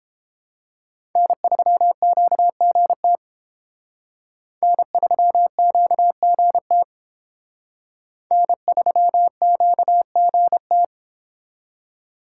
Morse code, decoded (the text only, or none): N3QGT N3QGT N3QGT